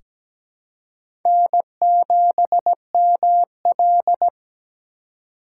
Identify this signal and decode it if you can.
Morse code, heard as a keyed tone, about 17 wpm, 700 Hz. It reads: N7ML
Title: K1USN SST QSO Practice - 17wpm